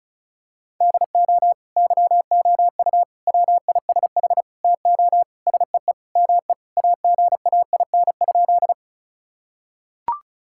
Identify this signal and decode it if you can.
Morse code, heard as a keyed tone, about 36 wpm, 700 Hz. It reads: DO YOU WISH TO SEE ME AGAIN?
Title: Sentences from Top 1600 Words 35wpm